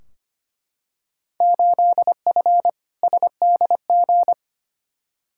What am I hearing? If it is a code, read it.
Morse code, heard as a keyed tone, about 25 wpm, 700 Hz. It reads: 8F SDG
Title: Field Day Exchanges Encore - 25wpm